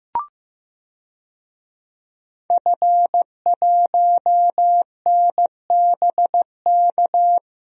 Morse code, heard as a keyed tone, about 15 wpm, 700 Hz. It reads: F1NBK
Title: DXCC-100 Call Signs - 5 Characters - Repeated 2x - 15wpm